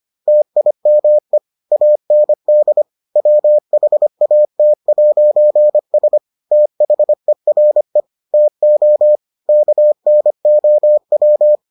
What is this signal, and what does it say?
Morse code, heard as a keyed tone, about 25 wpm, 600 Hz. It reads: TIME AND WHAT'S THERE TO KNOW